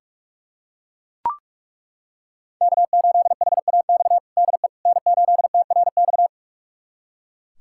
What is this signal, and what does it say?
Morse code, heard as a keyed tone, about 43 wpm, 700 Hz. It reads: K8HAX DE N8TRX
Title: USA Callsign DE Callsign - 5 Characters - 45wpm